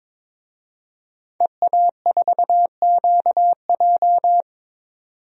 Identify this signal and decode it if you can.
Morse code, heard as a keyed tone, about 22 wpm, 700 Hz. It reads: EA4QJ
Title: DXCC-100 Call Signs - 5 Characters - 22wpm